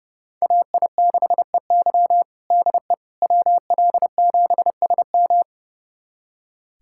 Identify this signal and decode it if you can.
Morse code, heard as a keyed tone, about 29 wpm, 700 Hz. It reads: AI6EY DE WL7SM
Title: USA Callsign DE Callsign - 5 Characters - 30wpm